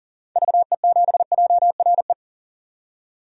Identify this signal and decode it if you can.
Morse code, heard as a keyed tone, about 40 wpm, 700 Hz. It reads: VE7JRE
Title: DXCC-100 Call Signs - 6 Characters - 40wpm